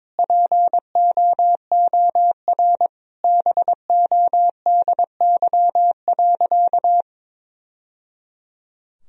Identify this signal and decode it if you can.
Morse code, heard as a keyed tone, about 22 wpm, 700 Hz. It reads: POOR BODY.
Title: Sentences from Top 1000 Words - Review Part 2 - 22wpm